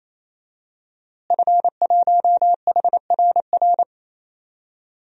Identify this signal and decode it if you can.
Morse code, heard as a keyed tone, about 28 wpm, 700 Hz. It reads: F1HRR
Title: DXCC-100 Call Signs - 5 Characters - Repeated 3x - 28wpm